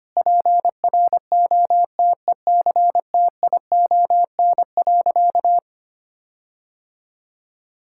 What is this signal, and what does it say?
Morse code, heard as a keyed tone, about 25 wpm, 700 Hz. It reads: PROTECTION.
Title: Sentences from Top 1600 Words 25wpm